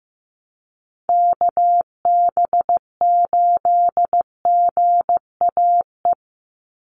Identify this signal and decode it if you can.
Morse code, heard as a keyed tone, about 15 wpm, 700 Hz. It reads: KB8GAE
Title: Call Signs Repeated 2x - 15wpm